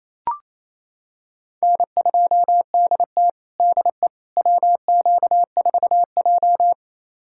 Morse code, heard as a keyed tone, about 28 wpm, 700 Hz. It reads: N2DT DE WQ4J
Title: USA Callsign DE Callsign - 4 Characters - 28wpm